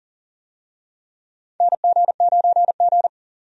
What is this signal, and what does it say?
Morse code, heard as a keyed tone, about 40 wpm, 700 Hz. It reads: NG9G
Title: USA Call Signs - 4 Characters - 40wpm